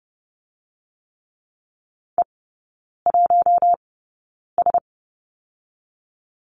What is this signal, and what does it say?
Morse code, heard as a keyed tone, about 30 wpm, spaced as for 9 wpm, 700 Hz. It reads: E1S
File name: Lesson 19e - Letter w - Three Characters 30wpm